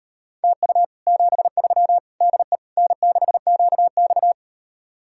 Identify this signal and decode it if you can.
Morse code, heard as a keyed tone, about 38 wpm, 700 Hz. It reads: TU 73 DE N6QX